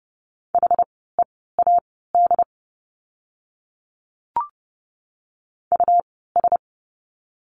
Morse code, heard as a keyed tone, about 30 wpm, spaced as for 17 wpm, 700 Hz. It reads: HEAD US
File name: Lesson 17f - Letter u - Words 30wpm